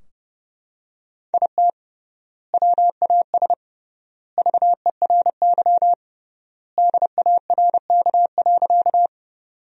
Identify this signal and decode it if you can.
Morse code, heard as a keyed tone, about 28 wpm, 700 Hz. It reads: IT WAS VERY DARK.
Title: 3x Word-Spacing - Sentences from Top 500 Words - 30wpm